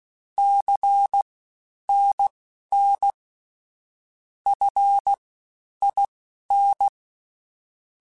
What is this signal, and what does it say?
Morse code, heard as a keyed tone, about 16 wpm, spaced as for 10 wpm, 800 Hz. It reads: CNN FIN
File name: CW Comparisons / CW Comparisons-mp3s / CW Comparisons Broken Characters @15